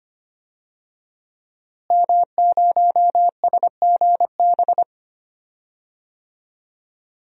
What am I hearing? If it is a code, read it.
Morse code, heard as a keyed tone, about 25 wpm, 700 Hz. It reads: M0SGB
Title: DXCC-100 Call Signs - 5 Characters - Repeated 3x - 25wpm